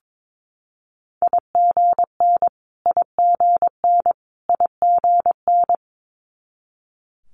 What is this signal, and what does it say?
Morse code, heard as a keyed tone, about 22 wpm, 700 Hz. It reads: IGN IGN IGN
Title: Multi-letter Phonemes Repeated 3x 22wpm